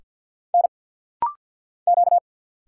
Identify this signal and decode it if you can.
Morse code, heard as a keyed tone, about 49 wpm, 700 Hz. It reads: N Q
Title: Single Letter-Number Alt - Rapid-Fire 50wpm